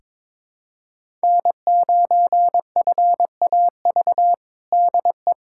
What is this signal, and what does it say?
Morse code, heard as a keyed tone, about 22 wpm, 700 Hz. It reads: N9FAV DE
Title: USA Callsign DE Callsign - 5 Characters - 22wpm